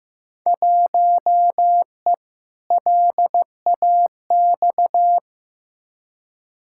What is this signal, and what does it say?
Morse code, heard as a keyed tone, about 15 wpm, 700 Hz. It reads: 1E LAX
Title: Field Day Exchanges - 15wpm